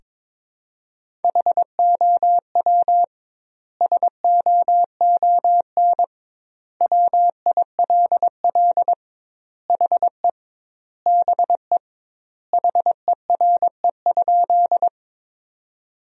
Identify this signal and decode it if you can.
Morse code, heard as a keyed tone, about 22 wpm, 700 Hz. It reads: HOW SOON WILL HE BE HERE?
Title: 2x Word-Spacing - Sentences from Top 400 Words - 22wpm